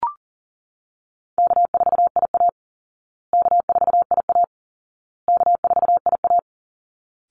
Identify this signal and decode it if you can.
Morse code, heard as a keyed tone, about 40 wpm, 700 Hz. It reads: K4IA K4IA K4IA
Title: Easy Call Signs Repeated 3x - 40wpm